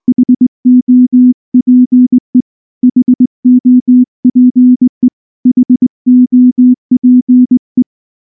Morse code, heard as a keyed tone, about 20 wpm, 250 Hz. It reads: HOPE HOPE HOPE
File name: morse / morse-training-20WPM-7